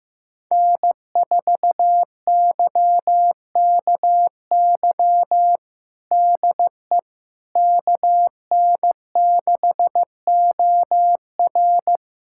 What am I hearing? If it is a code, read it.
Morse code, heard as a keyed tone, about 16 wpm, 700 Hz. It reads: N4YKY DE KN6OR
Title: USA Callsign DE Callsign 15wpm